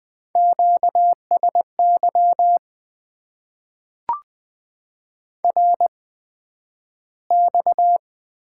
Morse code, heard as a keyed tone, about 20 wpm, spaced as for 11 wpm, 700 Hz. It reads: QSY RX
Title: Top 90 QSO Elements - 8x Character Spacing - 20wpm